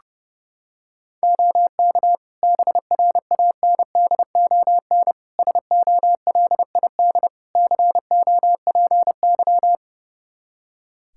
Morse code, heard as a keyed tone, about 30 wpm, 700 Hz. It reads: OK BRANDON SOLID COPY
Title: K1USN SST QSO Practice 30wpm